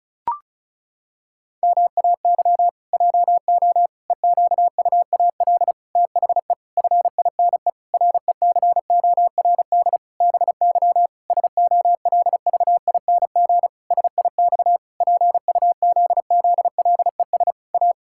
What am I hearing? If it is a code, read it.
Morse code, heard as a keyed tone, about 36 wpm, 700 Hz. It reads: MAY JO EQUAL THE FINE RECORD BY SOLVING SIX PUZZLES A
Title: Pangram Challenge v2 - 35wpm